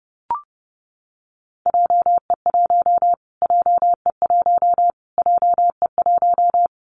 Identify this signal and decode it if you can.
Morse code, heard as a keyed tone, about 30 wpm, 700 Hz. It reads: JE1 JE1 JE1
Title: Callsign Prefixes for Top 25 DXCC - Repeated 3x - 30wpm